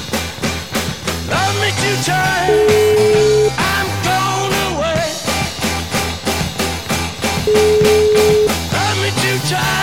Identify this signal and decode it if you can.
Ringback tone (Europe, 425 Hz), with noise.